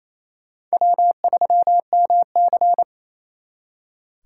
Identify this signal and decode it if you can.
Morse code, heard as a keyed tone, about 38 wpm, 700 Hz. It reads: W3MC